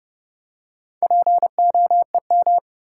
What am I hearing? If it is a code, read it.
Morse code, heard as a keyed tone, about 30 wpm, 700 Hz. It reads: POEM
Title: Four Letter Words - 30wpm